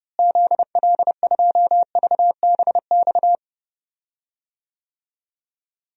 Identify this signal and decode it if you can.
Morse code, heard as a keyed tone, about 32 wpm, 700 Hz. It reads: ZL2VBX
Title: DXCC-100 Call Signs - 6 Characters - Repeated 2x - 30wpm